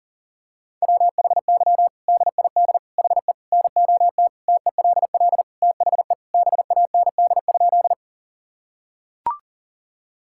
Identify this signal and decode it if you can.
Morse code, heard as a keyed tone, about 40 wpm, 700 Hz. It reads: WHY DID HE NOT TELL THE BAND?
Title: Sentences from Top 1600 Words 40wpm